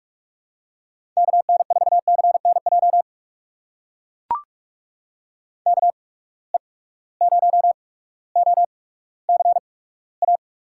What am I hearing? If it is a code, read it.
Morse code, heard as a keyed tone, about 42 wpm, spaced as for 19 wpm, 700 Hz. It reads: KN4CNJ KE0OCA
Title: USA Call Signs - 6 Characters - 8x Character Spacing - 45wpm